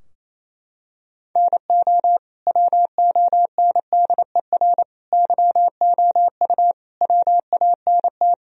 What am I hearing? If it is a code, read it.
Morse code, heard as a keyed tone, about 28 wpm, 700 Hz. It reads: NO WONDER YOU WANT
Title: Sentences from Top 1000 Words - Review Part 3 - 28wpm